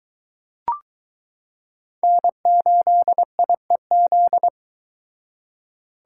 Morse code, heard as a keyed tone, about 23 wpm, 700 Hz. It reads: N8IEZ